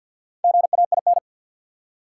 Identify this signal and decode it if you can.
Morse code, heard as a keyed tone, about 53 wpm, 700 Hz. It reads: GAIN
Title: Four Letter Words - 50wpm